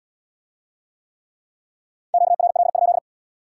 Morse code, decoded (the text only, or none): QRL?